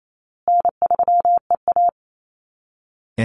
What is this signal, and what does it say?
Morse code, heard as a keyed tone, about 28 wpm, 700 Hz. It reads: N3EA
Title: Easy Call Signs Repeated 3x - 28wpm